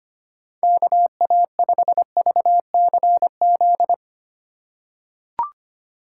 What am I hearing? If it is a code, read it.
Morse code, heard as a keyed tone, about 25 wpm, 700 Hz. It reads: KA5VCZ
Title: USA Call Signs - 6 Characters - 2x Character Spacing - 25wpm